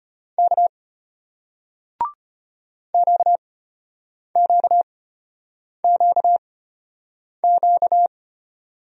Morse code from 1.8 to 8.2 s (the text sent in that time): QQQQ